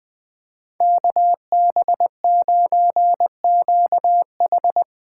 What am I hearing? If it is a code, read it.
Morse code, heard as a keyed tone, about 20 wpm, 700 Hz. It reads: KB9QH
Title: Speed Racing - USA 2x2 Call Signs 17wpm